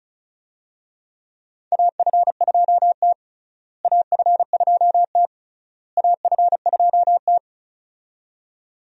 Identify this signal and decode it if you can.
Morse code, heard as a keyed tone, about 34 wpm, 700 Hz. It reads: AF2T AF2T AF2T
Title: USA Call Signs - 4 Characters - Repeated 3x - 35wpm